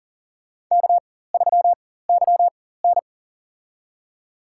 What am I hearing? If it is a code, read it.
Morse code, heard as a keyed tone, about 40 wpm, spaced as for 19 wpm, 700 Hz. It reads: K3YN
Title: USA Call Signs - 4x Character Spacing - 40wpm